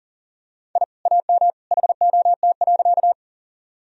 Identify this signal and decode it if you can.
Morse code, heard as a keyed tone, about 39 wpm, 700 Hz. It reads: I AM HOT.
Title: Sentences from Top 1000 Words - Review Part 3 - 40wpm